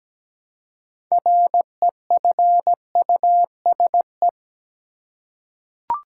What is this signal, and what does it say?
Morse code, heard as a keyed tone, about 17 wpm, 700 Hz. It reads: REFUSE
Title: Six Letter Words - 17wpm